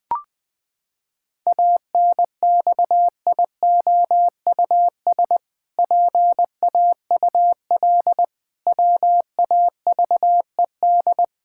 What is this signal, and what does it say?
Morse code, heard as a keyed tone, about 20 wpm, 700 Hz. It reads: ANXIOUS PAUL WAVED
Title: Pangram Challenge v2 - 20wpm